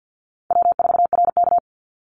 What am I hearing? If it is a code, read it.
Morse code, heard as a keyed tone, about 50 wpm, 700 Hz. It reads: W4RK
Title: NA SOTA Chaser Callsigns 50wpm